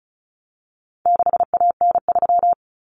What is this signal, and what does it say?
Morse code, heard as a keyed tone, about 34 wpm, 700 Hz. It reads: 6AN3
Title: Common Cut Numbers - 4-Digit 35wpm